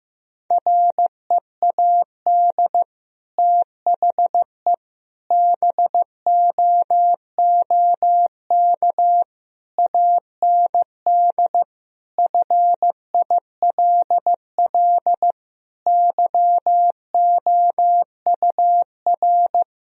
Morse code, heard as a keyed tone, about 14 wpm, 700 Hz. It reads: READ THE BOOK AND FILL YOUR
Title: Sentences from Top 1000 Words - Review Part 3 - 15wpm